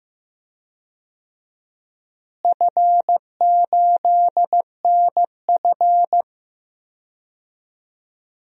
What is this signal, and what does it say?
Morse code, heard as a keyed tone, about 15 wpm, 700 Hz. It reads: F8NF
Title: DXCC-100 Call Signs - 4 Characters - Repeated 3x - 15wpm